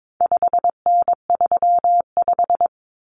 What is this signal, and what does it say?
Morse code, heard as a keyed tone, about 22 wpm, 700 Hz. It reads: 5N35
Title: Common Cut Numbers - 4-Digit 22wpm